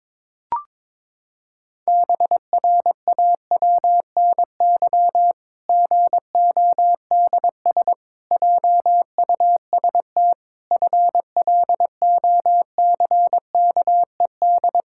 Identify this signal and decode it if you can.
Morse code, heard as a keyed tone, about 22 wpm, 700 Hz. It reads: BRAWNY GODS JUST FLOCKED
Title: Pangram Challenge v2 - 22wpm